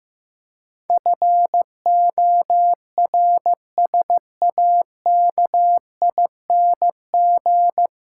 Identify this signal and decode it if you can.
Morse code, heard as a keyed tone, about 15 wpm, 700 Hz. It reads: FORSAKING
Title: Pangram Challenge v2 - 15wpm